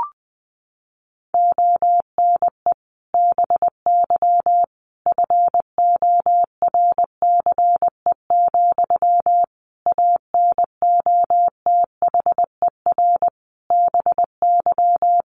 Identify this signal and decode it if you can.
Morse code, heard as a keyed tone, about 20 wpm, 700 Hz. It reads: ONE BY FORCE, ANOTHER BY
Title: Sentences from Top 1200 Words 20wpm